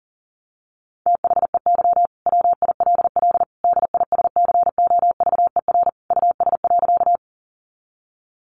Morse code, heard as a keyed tone, about 40 wpm, 700 Hz. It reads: THEY WILL DISCOVER US.
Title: Sentences from Top 1100 Words 40wpm